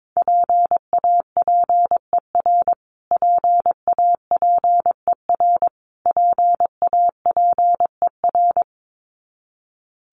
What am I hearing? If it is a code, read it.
Morse code, heard as a keyed tone, about 22 wpm, 700 Hz. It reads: PAPER PAPER PAPER
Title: Top 500 Words Repeated 22wpm